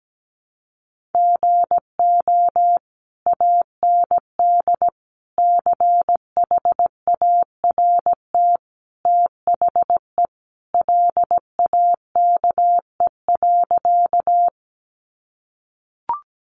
Morse code, heard as a keyed tone, about 17 wpm, 700 Hz. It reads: GO AND CHART THE LAKE.
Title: Sentences from Top 1300 Words 17wpm